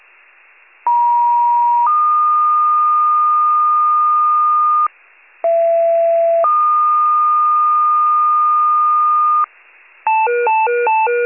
Two-tone page, heard as a one-second tone, then a three-second tone, about 950 then 1220 Hz, about 670 then 1220 Hz. It is heard from 0.9 to 9.4 s.
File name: wcalert